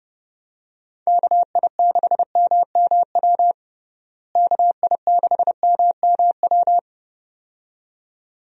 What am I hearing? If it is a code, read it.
Morse code, heard as a keyed tone, about 32 wpm, 700 Hz. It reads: KI6MMW KI6MMW
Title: USA Call Signs - 6 Characters - Repeated 2x - 30wpm